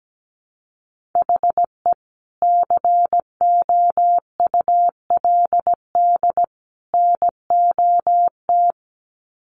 Morse code, heard as a keyed tone, about 17 wpm, 700 Hz. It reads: HE COULD NOT
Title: Sets of 3 Words 17wpm